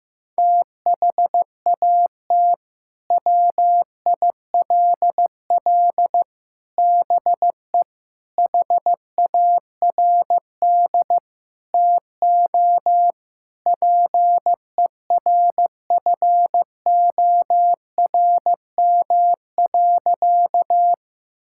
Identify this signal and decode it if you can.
Morse code, heard as a keyed tone, about 15 wpm, 700 Hz. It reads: THAT WILL BE HARD TO PERFORM.
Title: Sentences from Top 1000 Words - Review Part 5 - 15wpm